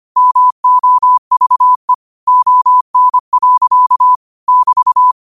Morse code, heard as a keyed tone, about 25 wpm, 1000 Hz. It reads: MOVE ON. <BT>